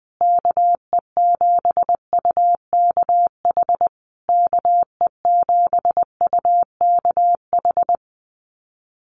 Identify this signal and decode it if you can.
Morse code, heard as a keyed tone, about 20 wpm, 700 Hz. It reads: KE7UKH KE7UKH
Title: USA Call Signs Repeated 3x 20wpm